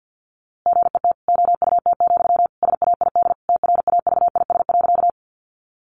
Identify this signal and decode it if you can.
Morse code, heard as a keyed tone, about 50 wpm, 700 Hz. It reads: GET OUT, SAID TRAVIS.
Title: A Sound Of Thunder 50wpm